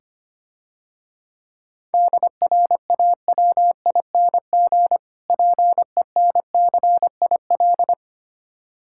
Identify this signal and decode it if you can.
Morse code, heard as a keyed tone, about 26 wpm, 700 Hz. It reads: DRAWING PENCIL
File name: practice / 25wpm / Difficult Words - Speed-Racing with Hints - 25wpm